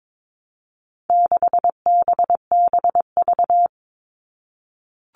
Morse code, heard as a keyed tone, about 22 wpm, 700 Hz. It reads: 6BBV
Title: All Cut Numbers - 4-Digit 22wpm